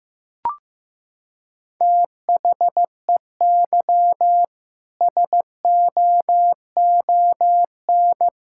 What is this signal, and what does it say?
Morse code, heard as a keyed tone, about 15 wpm, 700 Hz. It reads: THEY SOON
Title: Sentences from Top 1000 Words - Review Part 5 - 15wpm